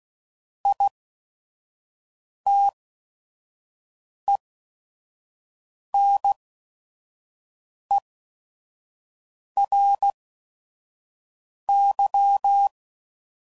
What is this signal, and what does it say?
Morse code, heard as a keyed tone, about 16 wpm, spaced as for 5 wpm, 800 Hz. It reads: ITENERY
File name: Prefixes / By First Letter / Is